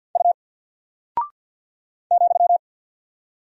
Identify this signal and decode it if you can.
Morse code, heard as a keyed tone, about 48 wpm, 700 Hz. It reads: U ,
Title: Single Character (Letter:Number:Punctuation:Prosign) 50wpm